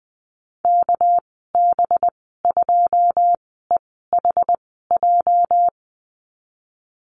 Morse code, heard as a keyed tone, about 20 wpm, spaced as for 15 wpm, 700 Hz. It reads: KB2EHJ
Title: 219 Lesson 35g - Number 0 - Call Signs 20wpm